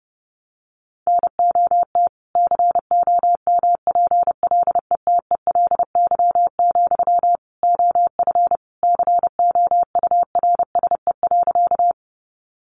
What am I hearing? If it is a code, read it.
Morse code, heard as a keyed tone, about 30 wpm, 700 Hz. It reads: NOT COMPLETELY, OF COURSE.
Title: Sentences from Top 1100 Words 30wpm